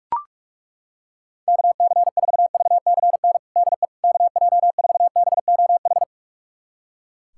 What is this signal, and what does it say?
Morse code, heard as a keyed tone, about 41 wpm, 700 Hz. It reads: KC4VCN DE KJ4BOH
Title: USA Callsign DE Callsign - 6 Characters - 45wpm